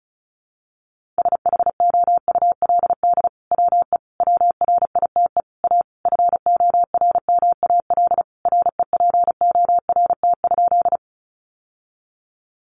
Morse code, heard as a keyed tone, about 34 wpm, 700 Hz. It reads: SHOULD WE WRITE A FORMAL REPORT?
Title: Sentences from Top 1300 Words 35wpm